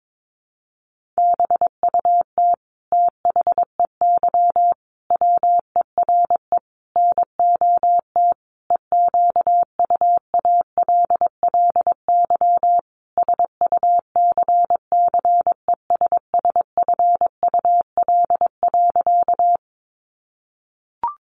Morse code, heard as a keyed tone, about 22 wpm, 700 Hz. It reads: BUT THEY WERE NOT EQUALLY SUCCESSFUL.
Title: Sentences from Top 1400 Words 22wpm